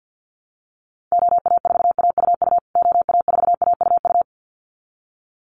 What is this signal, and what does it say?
Morse code, heard as a keyed tone, about 50 wpm, 700 Hz. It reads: OA4AUU OA4AUU
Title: USA Call Signs Repeated 2x 50wpm